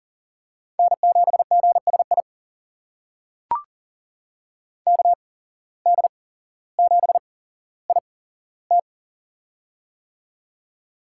Morse code, heard as a keyed tone, about 40 wpm, spaced as for 17 wpm, 700 Hz. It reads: N7GSI KD7IT
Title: USA Call Signs - 5 Characters - 8x Character Spacing - 40wpm